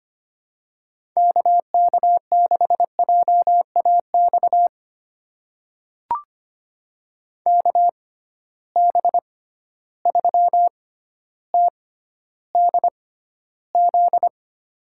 Morse code, heard as a keyed tone, about 25 wpm, spaced as for 13 wpm, 700 Hz. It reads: KK6JAX KB3TDZ
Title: USA Call Signs - 6 Characters - 6x Character Spacing - 25wpm